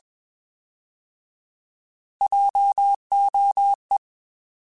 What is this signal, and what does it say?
Morse code, heard as a keyed tone, about 21 wpm, 800 Hz. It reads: JOE
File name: Alphabetical / Alphabetical mp3s / Js @20